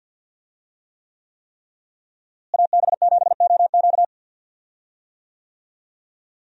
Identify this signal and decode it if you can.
Morse code, heard as a keyed tone, about 52 wpm, 700 Hz. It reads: AB7OQ
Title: USA Call Signs - 5 Characters - Repeated 3x - 50wpm